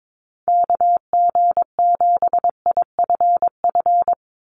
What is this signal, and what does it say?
Morse code, heard as a keyed tone, about 22 wpm, 700 Hz. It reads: KG7IFF
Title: USA Call Signs 22wpm